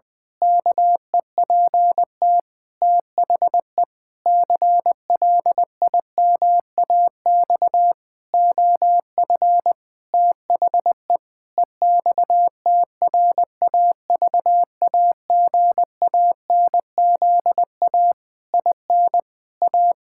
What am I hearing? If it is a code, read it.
Morse code, heard as a keyed tone, about 20 wpm, 700 Hz. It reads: KEPT THE CLIMAX OF THE EXTRAVAGANZA IN A
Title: Pangram Challenge v2 - 20wpm